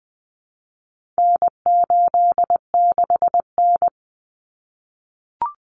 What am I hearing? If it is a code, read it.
Morse code, heard as a keyed tone, about 20 wpm, 700 Hz. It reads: N86N
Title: Common Cut Numbers - 4-Digit 20wpm